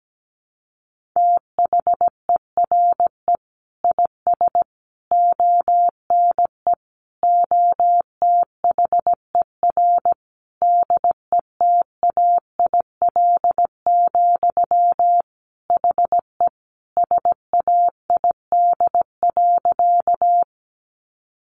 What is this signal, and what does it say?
Morse code, heard as a keyed tone, about 17 wpm, 700 Hz. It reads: THERE IS ONE OTHER DETAIL, HE SAID.
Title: Sentences from Top 1500 Words 17wpm